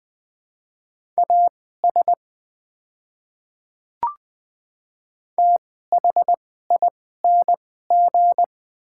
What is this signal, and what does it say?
Morse code, heard as a keyed tone, about 20 wpm, spaced as for 14 wpm, 700 Hz. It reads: AS THING
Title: 2x Character Spacing - Top 100 Words - 20wpm